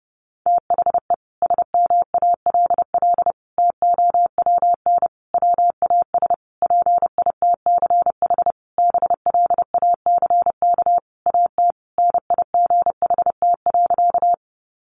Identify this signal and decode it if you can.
Morse code, heard as a keyed tone, about 30 wpm, 700 Hz. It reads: THE SMALL TOWN WAS PITCH BLACK AT NIGHT.
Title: Sentences from Top 1300 Words 30wpm